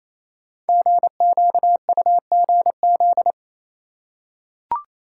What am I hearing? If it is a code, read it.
Morse code, heard as a keyed tone, about 29 wpm, 700 Hz. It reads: GQUGZ
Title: Code Groups - 5 Letters - 28wpm